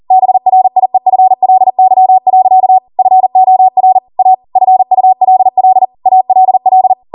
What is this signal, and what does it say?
Morse code, heard as a keyed tone, about 40 wpm, 750 Hz. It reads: BRIEFLY. FOR A FULL ALL
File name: Morse / 40WPM